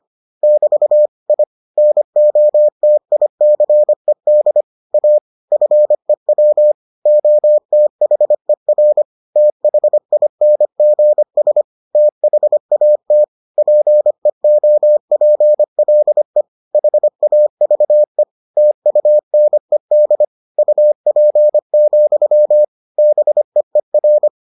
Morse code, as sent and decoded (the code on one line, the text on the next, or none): -...- / .. / -. --- - .. -.-. -.. / .- / ..-. .-- / --- - .... .-. / - .... .. -. --. ... / - .... .- - / .--. --- .--. .-.. / .... .- ...- . / - ..- -. -.. / ..- .--. --..-- / -... . .-.
<BT> I NOTICED A FEW OTHER THINGS THAT PEOPLE HAVE TUNED UP, BEER